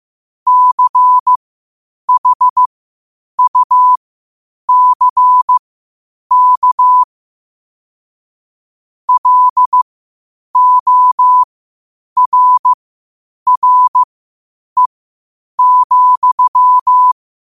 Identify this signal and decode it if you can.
Morse code, heard as a keyed tone, about 15 wpm, spaced as for 8 wpm, 1000 Hz. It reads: CHUCK LORRE,